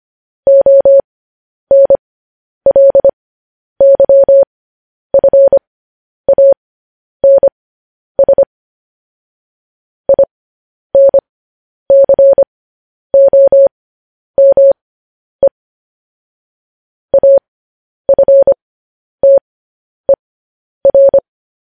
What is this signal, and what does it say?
Morse code, heard as a keyed tone, about 25 wpm, spaced as for 10 wpm, 550 Hz. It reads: ONLYFANS INCOME AFTER
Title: News Headlines at 10